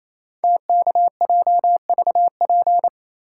Morse code, heard as a keyed tone, about 27 wpm, 700 Hz. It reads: TKJVP
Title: Code Groups - 5 Letters - 28wpm